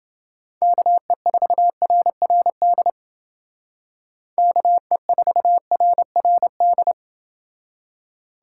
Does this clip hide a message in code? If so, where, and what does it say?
Yes: Morse code from 0.6 to 6.9 s, heard as a keyed tone, about 28 wpm, 700 Hz. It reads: KE4RRD KE4RRD